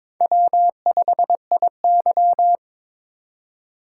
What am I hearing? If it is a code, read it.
Morse code, heard as a keyed tone, about 23 wpm, 700 Hz. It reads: W5IY
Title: USA Call Signs - 4 Characters - 3x Character Spacing - 22wpm